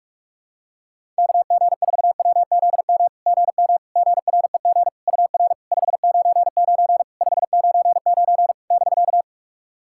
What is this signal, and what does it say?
Morse code, heard as a keyed tone, about 44 wpm, 690 Hz. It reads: KG4WZM GM GREG UR 599 599 <BK>
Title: SOTA QSO Practice 45wpm